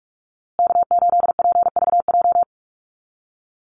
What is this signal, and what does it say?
Morse code, heard as a keyed tone, about 44 wpm, 700 Hz. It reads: K8PVJ
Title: USA Call Signs Repeated 3x 45wpm